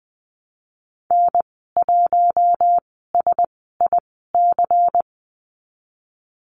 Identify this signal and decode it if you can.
Morse code, heard as a keyed tone, about 20 wpm, spaced as for 15 wpm, 700 Hz. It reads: N1SIC
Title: 122 Lesson 20g - Number 3 - Call Signs 20wpm